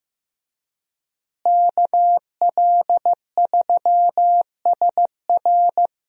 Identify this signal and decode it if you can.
Morse code, heard as a keyed tone, about 15 wpm, 700 Hz. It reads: KL3SR
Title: USA Call Signs - 5 Characters - Repeated 3x - 15wpm